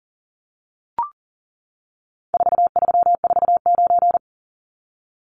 Morse code, heard as a keyed tone, about 40 wpm, 700 Hz. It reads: 4349
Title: Numbers - 4-Digit 40wpm